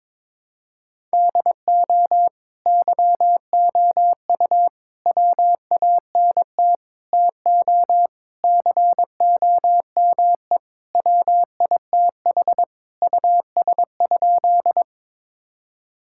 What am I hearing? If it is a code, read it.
Morse code, heard as a keyed tone, about 21 wpm, 700 Hz. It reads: DO YOU WANT TO COME WITH US?
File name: practice / 22wpm / Sentences with 3-Word Phrases - 22wpm